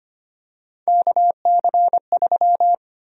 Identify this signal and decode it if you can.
Morse code, heard as a keyed tone, about 25 wpm, 700 Hz. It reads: KC3
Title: USA Call Sign Prefixes - Repeated 2x - 25wpm